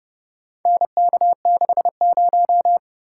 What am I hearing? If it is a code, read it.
Morse code, heard as a keyed tone, about 28 wpm, 700 Hz. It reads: NK60